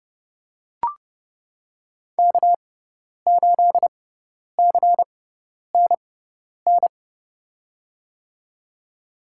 Morse code, heard as a keyed tone, about 32 wpm, spaced as for 10 wpm, 700 Hz. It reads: K8CNN
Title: USA Call Signs - 5 Characters - 6x Character Spacing - 30wpm